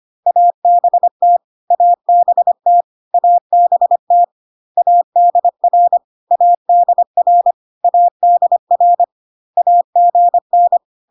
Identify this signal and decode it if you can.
Morse code, heard as a keyed tone, about 25 wpm, 700 Hz. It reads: ABT ABT ABT ADR ADR ADR AGN
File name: CW abbreviations alphabetical repeated 3 times 25 25wpm